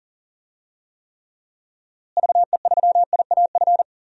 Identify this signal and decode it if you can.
Morse code, heard as a keyed tone, about 40 wpm, 700 Hz. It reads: VE3IAF
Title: DXCC-100 Call Signs - 6 Characters - Repeated 2x - 40wpm